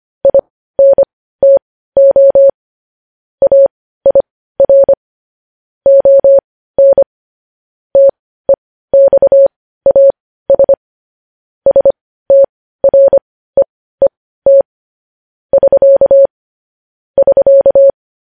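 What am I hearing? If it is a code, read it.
Morse code, heard as a keyed tone, about 25 wpm, spaced as for 15 wpm, 550 Hz. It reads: INTO AIR ON TEXAS STREET <SK> <SK>
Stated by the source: Morse code news headlines sent at 15, opening with vvv.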